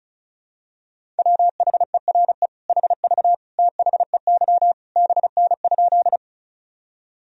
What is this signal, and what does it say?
Morse code, heard as a keyed tone, about 36 wpm, 700 Hz. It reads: WHERE HV THEY BN?
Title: Sentences from Top 200 Words and Abbreviations 35wpm